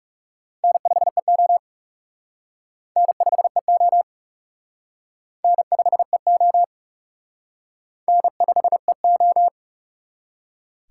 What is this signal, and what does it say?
Morse code, heard as a keyed tone, about 36 wpm, 700 Hz. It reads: N5EO N5EO N5EO N5EO
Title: Speed Racing - USA 1x2 Call Signs 30wpm